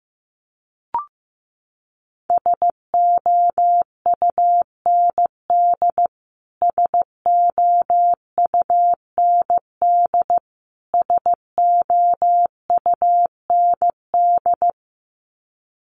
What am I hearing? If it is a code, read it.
Morse code, heard as a keyed tone, about 14 wpm, 700 Hz. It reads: SOUND SOUND SOUND
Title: Top 300 Words Repeated 15wpm